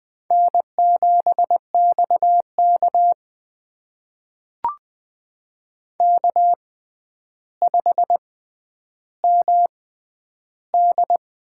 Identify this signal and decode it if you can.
Morse code, heard as a keyed tone, about 21 wpm, spaced as for 10 wpm, 700 Hz. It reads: N7XK K5MD
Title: USA Call Signs - 4 Characters - 6x Character Spacing - 20wpm